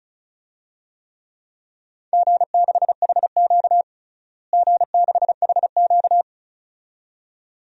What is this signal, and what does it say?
Morse code, heard as a keyed tone, about 36 wpm, 700 Hz. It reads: G6HQ G6HQ
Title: DXCC-100 Call Signs - 4 Characters - Repeated 2x - 35wpm